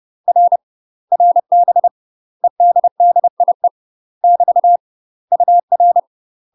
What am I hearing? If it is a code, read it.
Morse code, heard as a keyed tone, about 30 wpm, 700 Hz. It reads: R RB EDDIE <BT> UR
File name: UK MASTER QSOs TRAINING / 30WPM / UK MASTER QSOs 0038